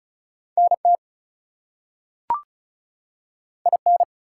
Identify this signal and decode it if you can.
Morse code, heard as a keyed tone, about 36 wpm, 700 Hz. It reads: NT IN
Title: US States and Canadian Provinces 35wpm